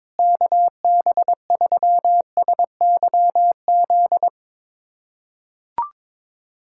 Morse code, heard as a keyed tone, about 22 wpm, 700 Hz. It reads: KB3SYZ
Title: USA Callsign DE Callsign 22wpm